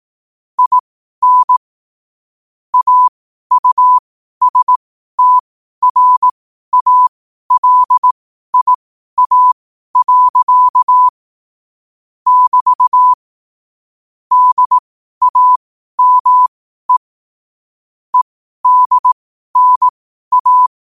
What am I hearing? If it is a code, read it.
Morse code, heard as a keyed tone, about 18 wpm, spaced as for 12 wpm, 1000 Hz. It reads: IN AUSTRALIA. <BT> DAME EDNA
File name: qotdfiles / 12wpm